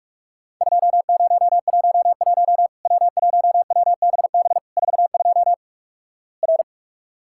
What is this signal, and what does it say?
Morse code, heard as a keyed tone, about 45 wpm, 700 Hz. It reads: 2011 W1WBB 42 R
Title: Worked All Europe DX Contest - QTC Exchange - 45wpm